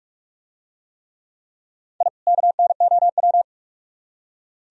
Morse code, heard as a keyed tone, about 44 wpm, 700 Hz. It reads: I KNOW